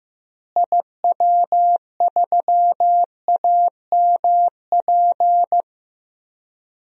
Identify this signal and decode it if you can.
Morse code, heard as a keyed tone, about 15 wpm, 700 Hz. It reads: IW3AMP
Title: DXCC-100 Call Signs - 6 Characters - Repeated 3x - 15wpm